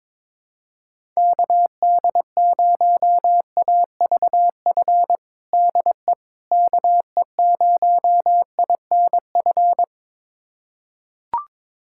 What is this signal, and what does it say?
Morse code, heard as a keyed tone, about 22 wpm, 700 Hz. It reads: KD0AVF DE KE0INF
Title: USA Callsign DE Callsign 22wpm